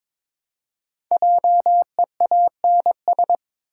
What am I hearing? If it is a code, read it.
Morse code, heard as a keyed tone, about 22 wpm, 700 Hz. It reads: JEANS
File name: practice / 22wpm / Difficult Words with Hints - 22wpm